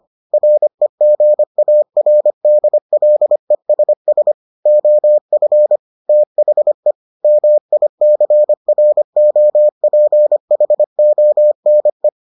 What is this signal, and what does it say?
Morse code, heard as a keyed tone, about 25 wpm, 600 Hz. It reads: REGARDLESS OF THE MICROPHONE